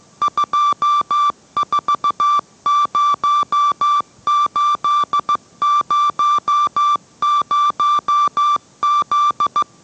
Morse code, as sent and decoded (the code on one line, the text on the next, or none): ..--- ....- ----- ---.. ----- ----- --..
240800Z